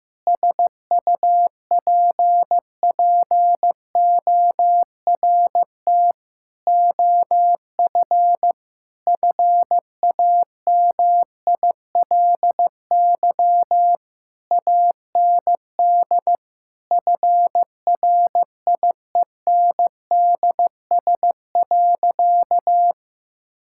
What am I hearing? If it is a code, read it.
Morse code, heard as a keyed tone, about 15 wpm, 700 Hz. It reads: SUPPORT OF FAMILY AND FRIENDS.
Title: Sentences from Top 1000 Words - Review Part 5 - 15wpm